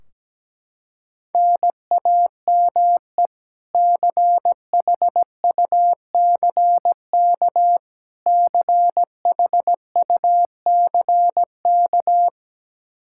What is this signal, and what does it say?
Morse code, heard as a keyed tone, about 17 wpm, 700 Hz. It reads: NAME CHUCK CHUCK
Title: K1USN SST QSO Practice 17wpm